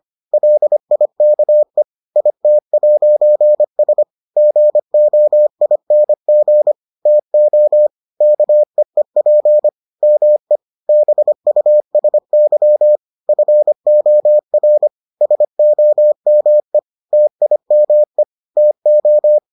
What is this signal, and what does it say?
Morse code, heard as a keyed tone, about 25 wpm, 600 Hz. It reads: LIKE IT'S GOING TO KEEP ME BUSY FOR SOME TIME TO